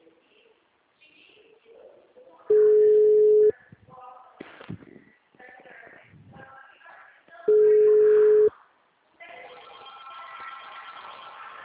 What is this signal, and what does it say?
Ringback tone (Europe, 425 Hz).